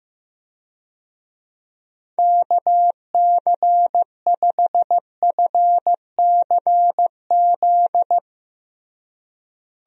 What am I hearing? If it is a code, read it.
Morse code, heard as a keyed tone, about 15 wpm, 700 Hz. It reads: KC5FCZ
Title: USA Call Signs - 6 Characters - Repeated 2x - 15wpm